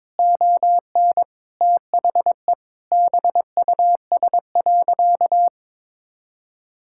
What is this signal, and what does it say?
Morse code, heard as a keyed tone, about 22 wpm, 700 Hz. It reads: ON THE BUS.
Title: Sentences from Top 1000 Words - Review Part 6 - 22wpm